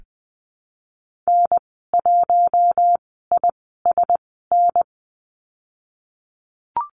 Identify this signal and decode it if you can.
Morse code, heard as a keyed tone, about 19 wpm, spaced as for 14 wpm, 700 Hz. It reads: N1ISN
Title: 055 Lesson 10g - Letter r - Call Signs 20wpm